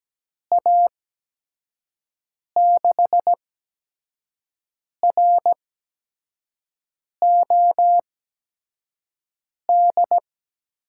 Morse code, heard as a keyed tone, about 17 wpm, 700 Hz. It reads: A 6 R O D
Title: USA Call Signs - 6 Characters - 8x Character Spacing - 17wpm